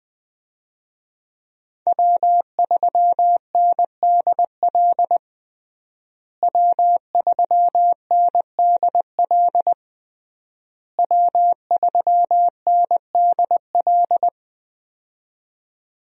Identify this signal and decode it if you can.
Morse code, heard as a keyed tone, about 20 wpm, 700 Hz. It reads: W3NDL W3NDL W3NDL
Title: USA Call Signs - 5 Characters - Repeated 3x - 20wpm